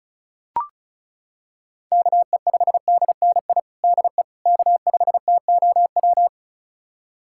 Morse code, heard as a keyed tone, about 34 wpm, 700 Hz. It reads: KE5DNI DE K5TOW
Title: USA Callsign DE Callsign 35wpm